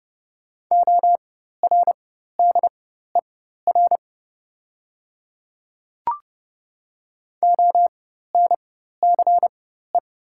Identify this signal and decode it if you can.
Morse code, heard as a keyed tone, about 30 wpm, spaced as for 14 wpm, 700 Hz. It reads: ORDER ONCE
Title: 4x Character Spacing - Top 500 Words - 30wpm